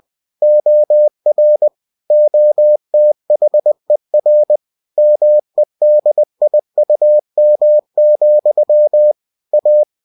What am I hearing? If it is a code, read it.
Morse code, heard as a keyed tone, about 20 wpm, 600 Hz. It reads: OR OTHER MEDIUM, A